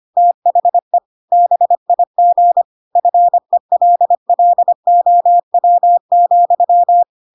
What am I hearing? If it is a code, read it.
Morse code, heard as a keyed tone, about 25 wpm, 700 Hz. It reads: THE BIG FELLOW,
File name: Pinocchio / Pinocchio ch3 25 25 ik1hge